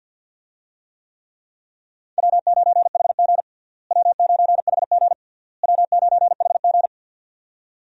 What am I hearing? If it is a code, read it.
Morse code, heard as a keyed tone, about 52 wpm, 700 Hz. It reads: W9HG W9HG W9HG
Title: USA Call Signs - 4 Characters - Repeated 3x - 50wpm